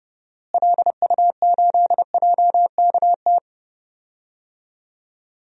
Morse code, heard as a keyed tone, about 30 wpm, 700 Hz. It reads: LU8JKT
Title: DXCC-100 Call Signs - 6 Characters - Repeated 3x - 30wpm